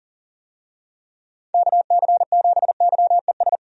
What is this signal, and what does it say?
Morse code, heard as a keyed tone, about 40 wpm, 700 Hz. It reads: KC7YES
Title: USA Call Signs - 6 Characters - 40wpm